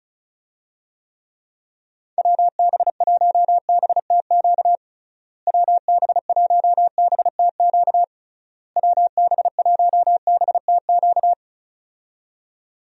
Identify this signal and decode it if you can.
Morse code, heard as a keyed tone, about 34 wpm, 700 Hz. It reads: WB1BTQ WB1BTQ WB1BTQ
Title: USA Call Signs - 6 Characters - Repeated 3x - 35wpm